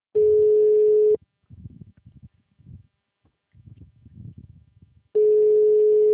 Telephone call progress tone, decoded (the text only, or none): ringback tone